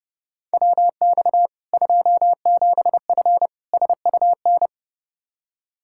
Morse code, heard as a keyed tone, about 30 wpm, 700 Hz. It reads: WX 27F SUN
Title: Practice QSO - Exchange 2 - Info - 30wpm